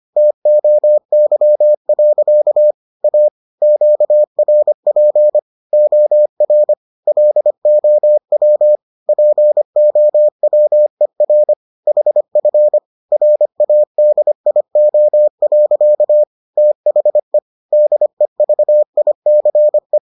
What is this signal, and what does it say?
Morse code, heard as a keyed tone, about 25 wpm, 600 Hz. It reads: TOY. A QRP OR LOW POWER HF RADIO. THE DEVICE